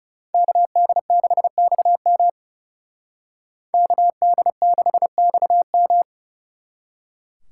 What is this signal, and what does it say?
Morse code, heard as a keyed tone, about 39 wpm, 700 Hz. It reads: KD6XM KD6XM